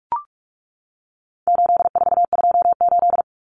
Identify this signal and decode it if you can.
Morse code, heard as a keyed tone, about 44 wpm, 700 Hz. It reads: 8428
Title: Numbers - 4-Digit 45wpm